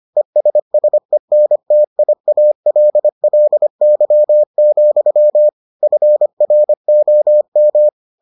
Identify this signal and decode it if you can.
Morse code, heard as a keyed tone, about 25 wpm, 600 Hz. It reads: ESSENTIALLY, FROM